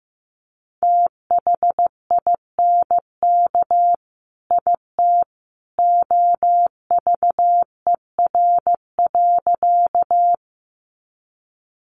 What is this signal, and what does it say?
Morse code, heard as a keyed tone, about 15 wpm, 700 Hz. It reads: THINK IT OVER.
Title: Sentences from Top 200 Words 15wpm